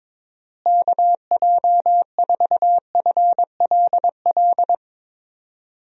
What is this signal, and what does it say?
Morse code, heard as a keyed tone, about 21 wpm, 700 Hz. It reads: KJ4FLL
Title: USA Call Signs - 6 Characters - Repeated 3x - 22wpm